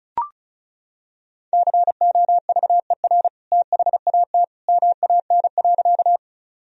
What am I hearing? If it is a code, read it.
Morse code, heard as a keyed tone, about 36 wpm, 700 Hz. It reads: COVER THAT MAN.
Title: Sentences from Top 1000 Words - Review Part 1 - 35wpm